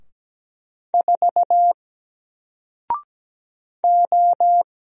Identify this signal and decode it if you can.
Morse code, heard as a keyed tone, about 17 wpm, 700 Hz. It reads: 4 O
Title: Single Letter-Number Alt 17wpm